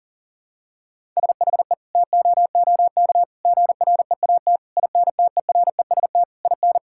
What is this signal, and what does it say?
Morse code, heard as a keyed tone, about 40 wpm, 700 Hz. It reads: SHE TOOK GREAT INTEREST IN
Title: Sentences from Top 1000 Words - Review Part 6 - 40wpm